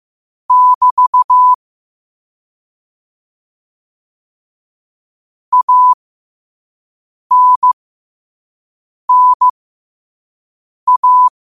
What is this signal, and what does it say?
Morse code, heard as a keyed tone, about 15 wpm, spaced as for 5 wpm, 1000 Hz. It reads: <BT> ANNA
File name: qotdfiles / 5wpm